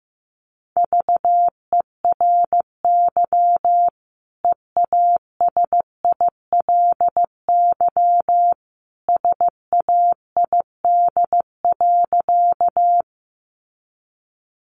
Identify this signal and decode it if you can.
Morse code, heard as a keyed tone, about 15 wpm, 700 Hz. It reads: VERY EASILY SAID.
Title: Sentences from Top 900 Words 15wpm